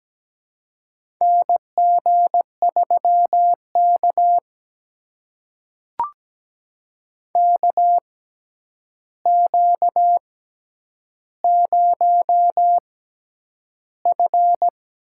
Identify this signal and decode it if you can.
Morse code, heard as a keyed tone, about 17 wpm, spaced as for 9 wpm, 700 Hz. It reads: NG3K KQ0F
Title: USA Call Signs - 4 Characters - 6x Character Spacing - 17wpm